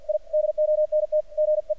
Morse code, heard as a keyed tone, about 46 wpm, 610 Hz. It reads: I DONT GE
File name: archived / CRSnet